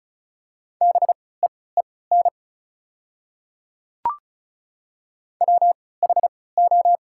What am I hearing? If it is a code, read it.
Morse code, heard as a keyed tone, about 36 wpm, spaced as for 20 wpm, 690 Hz. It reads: BEEN WHO
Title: 3x Character Spacing - Top 100 Words - 35wpm